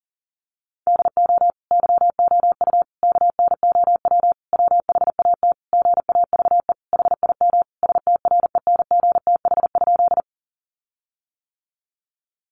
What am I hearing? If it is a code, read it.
Morse code, heard as a keyed tone, about 40 wpm, 700 Hz. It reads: DO YOU KNOW WHAT GAVE HIM STRENGTH?
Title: Sentences from Top 1300 Words 40wpm